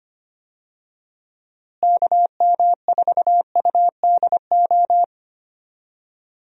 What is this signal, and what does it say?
Morse code, heard as a keyed tone, about 25 wpm, 700 Hz. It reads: KM4UDO
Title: USA Call Signs - 6 Characters - Repeated 2x - 25wpm